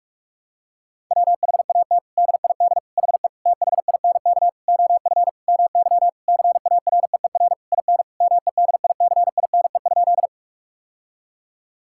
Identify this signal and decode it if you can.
Morse code, heard as a keyed tone, about 46 wpm, 700 Hz. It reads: WHAT DID HE THINK OF MY CAREER IN MEDICINE?
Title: Sentences from Top 1000 Words - Review Part 6 - 45wpm